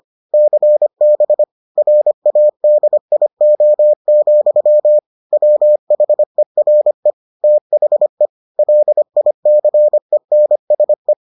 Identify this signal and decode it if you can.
Morse code, heard as a keyed tone, about 25 wpm, 600 Hz. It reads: CB RADIO, WHERE THE LICENSE